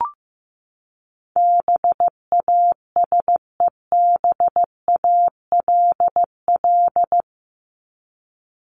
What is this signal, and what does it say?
Morse code, heard as a keyed tone, about 15 wpm, 700 Hz. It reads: BASEBALL
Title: Top 1751-2000 Words 15wpm